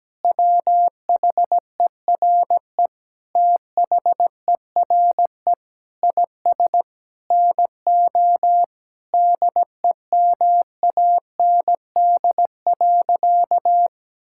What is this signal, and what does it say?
Morse code, heard as a keyed tone, about 17 wpm, 700 Hz. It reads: WHERE THERE IS NO DEMAND.
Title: Sentences from Top 1000 Words - Review Part 5 - 17wpm